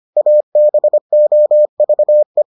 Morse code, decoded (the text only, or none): ABOVE